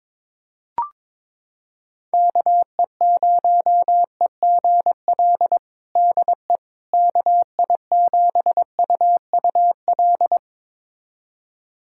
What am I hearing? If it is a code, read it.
Morse code, heard as a keyed tone, about 22 wpm, 700 Hz. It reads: KE0EGL DE KI7UUL
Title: USA Callsign DE Callsign 22wpm